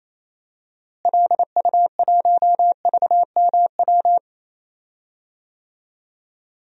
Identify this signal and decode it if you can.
Morse code, heard as a keyed tone, about 28 wpm, 700 Hz. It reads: LU1VMW
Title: DXCC-100 Call Signs - 6 Characters - Repeated 3x - 28wpm